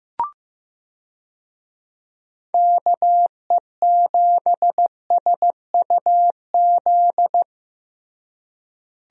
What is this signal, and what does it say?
Morse code, heard as a keyed tone, about 15 wpm, 700 Hz. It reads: KE7SUZ
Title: USA Call Signs - 6 Characters - Repeated 2x - 15wpm